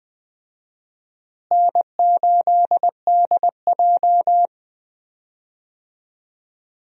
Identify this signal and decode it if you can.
Morse code, heard as a keyed tone, about 20 wpm, 700 Hz. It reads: N8DJ
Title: USA Call Signs - 4 Characters - Repeated 2x - 20wpm